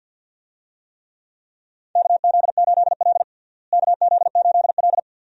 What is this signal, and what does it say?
Morse code, heard as a keyed tone, about 52 wpm, 700 Hz. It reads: KZ8L KZ8L
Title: USA Call Signs - 4 Characters - Repeated 3x - 50wpm